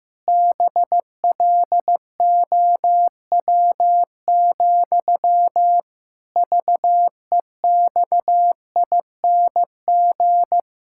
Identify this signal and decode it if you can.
Morse code, heard as a keyed tone, about 15 wpm, 700 Hz. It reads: BLOW, VEXING
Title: Pangram Challenge v2 - 15wpm